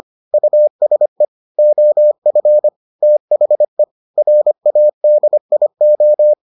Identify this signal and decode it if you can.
Morse code, heard as a keyed tone, about 25 wpm, 600 Hz. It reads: USE OF THE RADIO